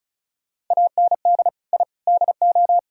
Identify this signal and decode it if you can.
Morse code, heard as a keyed tone, about 44 wpm, 700 Hz. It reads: AND I DO